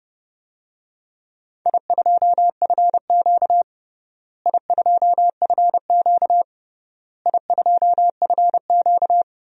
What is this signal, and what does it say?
Morse code, heard as a keyed tone, about 30 wpm, 700 Hz. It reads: I2FQ I2FQ I2FQ
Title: DXCC-100 Call Signs - 4 Characters - Repeated 3x - 30wpm